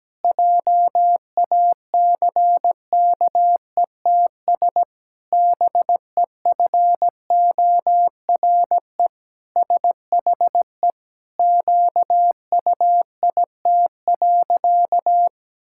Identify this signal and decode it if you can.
Morse code, heard as a keyed tone, about 17 wpm, 700 Hz. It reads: JACKETS BEFORE SHE QUIT.
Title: Pangram Challenge v2 - 17wpm